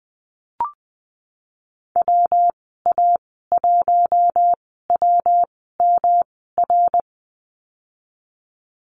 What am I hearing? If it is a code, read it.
Morse code, heard as a keyed tone, about 20 wpm, spaced as for 15 wpm, 700 Hz. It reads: WA1WMR
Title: 122 Lesson 20g - Number 3 - Call Signs 20wpm